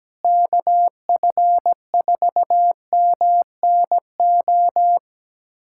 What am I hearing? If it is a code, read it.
Morse code, heard as a keyed tone, about 17 wpm, 700 Hz. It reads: KF4MNO
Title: USA Callsign DE Callsign - 6 Characters - 17wpm